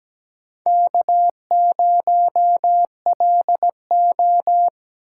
Morse code, heard as a keyed tone, about 17 wpm, 700 Hz. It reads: K0LO
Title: USA Call Signs - 4 Characters - 4x Character Spacing - 17wpm